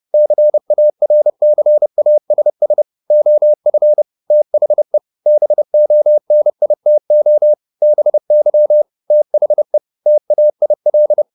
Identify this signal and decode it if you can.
Morse code, heard as a keyed tone, about 30 wpm, 600 Hz. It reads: CARCASS OF THE BONITO BY THE TAIL
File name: the old man and the sea / the old man and the sea 1 30wpm